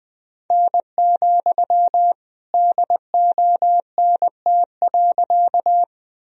Morse code, heard as a keyed tone, about 20 wpm, 700 Hz. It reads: N, DONT.
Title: Sentences from Top 300 Words and Abbreviations 20wpm